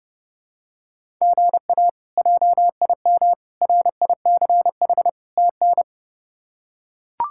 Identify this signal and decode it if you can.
Morse code, heard as a keyed tone, about 32 wpm, 700 Hz. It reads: GA JIM RICH TN
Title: K1USN SST QSO Practice - 30wpm